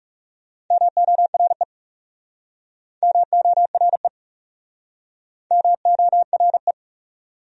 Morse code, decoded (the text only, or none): MORE MORE MORE